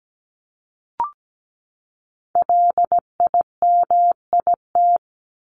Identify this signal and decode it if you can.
Morse code, heard as a keyed tone, about 17 wpm, 700 Hz. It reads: LIMIT
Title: Top 1251-1500 Words 17wpm